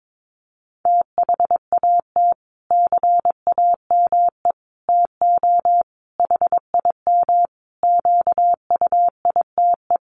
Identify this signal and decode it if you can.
Morse code, heard as a keyed tone, about 22 wpm, 700 Hz. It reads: THAT CAME TO HIM QUITE
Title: Sentences from Top 1300 Words 22wpm